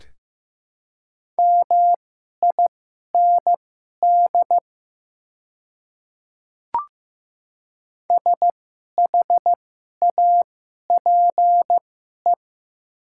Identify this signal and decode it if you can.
Morse code, heard as a keyed tone, about 15 wpm, spaced as for 11 wpm, 700 Hz. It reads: MIND SHAPE
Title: 2x Character Spacing - Top 500 Words - 15wpm